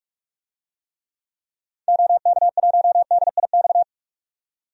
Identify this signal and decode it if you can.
Morse code, heard as a keyed tone, about 46 wpm, 690 Hz. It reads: OK1DIX
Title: DXCC-100 Call Signs - 6 Characters - 45wpm